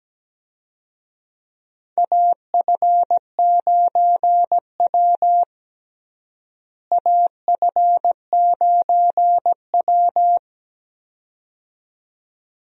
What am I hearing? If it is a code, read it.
Morse code, heard as a keyed tone, about 17 wpm, 700 Hz. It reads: AF9W AF9W
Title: USA Call Signs - 4 Characters - Repeated 2x - 17wpm